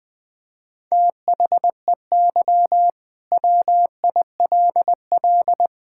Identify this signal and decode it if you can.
Morse code, heard as a keyed tone, about 21 wpm, 700 Hz. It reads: THEY WILL
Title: Sentences from Top 1000 Words - Review Part 5 - 20wpm